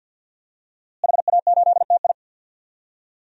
Morse code, heard as a keyed tone, about 46 wpm, 700 Hz. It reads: HA8TI
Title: DXCC-100 Call Signs - 5 Characters - Repeated 3x - 50wpm